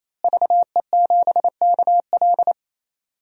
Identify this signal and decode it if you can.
Morse code, heard as a keyed tone, about 28 wpm, 700 Hz. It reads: VE7KL
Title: DXCC-100 Call Signs - 5 Characters - Repeated 3x - 28wpm